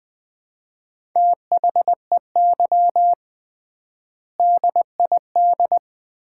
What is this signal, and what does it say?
Morse code, heard as a keyed tone, about 20 wpm, 700 Hz. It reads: THEY DID
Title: 3x Word-Spacing - Sets of 3 Words - 20wpm